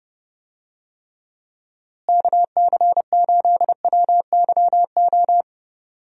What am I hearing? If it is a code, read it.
Morse code, heard as a keyed tone, about 30 wpm, 700 Hz. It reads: KC8WYO
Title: USA Call Signs - 6 Characters - Repeated 2x - 30wpm